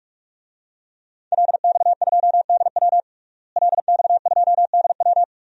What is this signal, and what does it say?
Morse code, heard as a keyed tone, about 43 wpm, 700 Hz. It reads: LX2DW LX2DW
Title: DXCC-100 Call Signs - 5 Characters - Repeated 2x - 45wpm